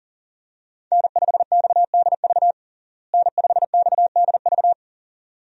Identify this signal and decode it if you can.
Morse code, heard as a keyed tone, about 37 wpm, 700 Hz. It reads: N5XDV N5XDV
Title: USA Call Signs - 5 Characters - Repeated 2x - 40wpm